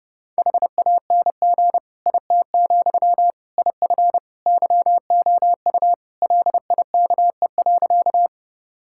Morse code, heard as a keyed tone, about 30 wpm, 700 Hz. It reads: HANG IT, IF YOU LIKE.
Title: Sentences from Top 1600 Words 30wpm